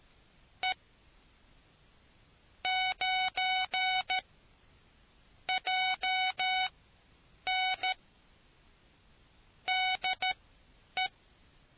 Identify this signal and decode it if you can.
Morse code, heard as a keyed tone, about 13 wpm, spaced as for 8 wpm, 750 Hz. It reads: E 9 JN DE